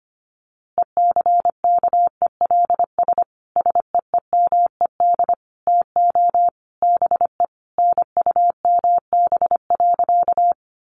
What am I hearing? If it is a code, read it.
Morse code, heard as a keyed tone, about 25 wpm, 700 Hz. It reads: ECKELS SEEMED TO BE NUMB.
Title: A Sound Of Thunder 25wpm